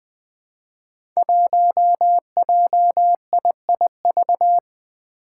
Morse code, heard as a keyed tone, about 20 wpm, 700 Hz. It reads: 1JIIV
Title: Five Challenging Letter-Number Combinations 20wpm